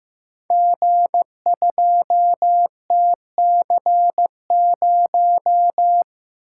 Morse code, heard as a keyed tone, about 15 wpm, 700 Hz. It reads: G2TC0
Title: Code Groups - 5 Letter:Numbers - 15wpm